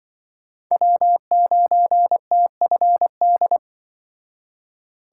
Morse code, heard as a keyed tone, about 24 wpm, 700 Hz. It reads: W9TFD
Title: Speed Racing - USA 1x3 Call Signs 20wpm